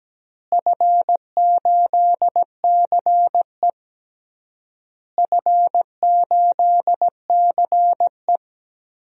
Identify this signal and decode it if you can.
Morse code, heard as a keyed tone, about 17 wpm, 700 Hz. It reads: F8CE F8CE
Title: DXCC-100 Call Signs - 4 Characters - Repeated 3x - 17wpm